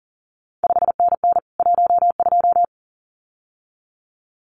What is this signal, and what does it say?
Morse code, heard as a keyed tone, about 41 wpm, 700 Hz. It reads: 5NN 12
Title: CQ WW DX Contest Exchange Encore - 40wpm